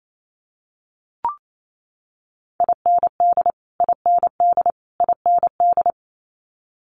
Morse code, heard as a keyed tone, about 28 wpm, 700 Hz. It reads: IND IND IND
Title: Multi-letter Phonemes Repeated 3x 28wpm